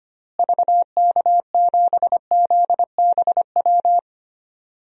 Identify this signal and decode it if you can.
Morse code, heard as a keyed tone, about 25 wpm, 700 Hz. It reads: VK7ZBW
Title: DXCC-100 Call Signs - 6 Characters - Repeated 2x - 25wpm